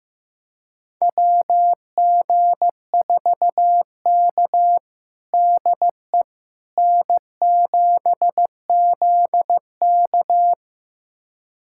Morse code, heard as a keyed tone, about 15 wpm, 700 Hz. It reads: WG4K DE N7ZK
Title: USA Callsign DE Callsign - 4 Characters - 15wpm